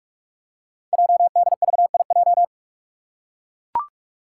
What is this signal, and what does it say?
Morse code, heard as a keyed tone, about 44 wpm, 690 Hz. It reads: JDVIJ
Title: Code Groups - 5 Letter:Numbers - 45wpm